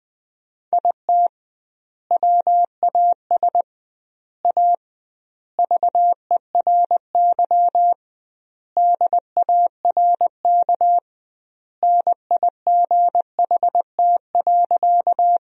Morse code, heard as a keyed tone, about 20 wpm, 700 Hz. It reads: IT WAS A VERY DARK NIGHT.
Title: 2x Word-Spacing - Sentences from Top 500 Words - 20wpm